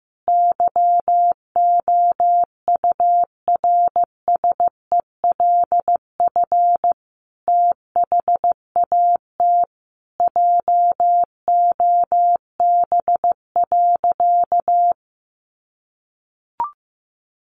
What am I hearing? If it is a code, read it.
Morse code, heard as a keyed tone, about 14 wpm, 700 Hz. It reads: YOURSELF THAT JOB.
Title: Sentences from Top 1100 Words 15wpm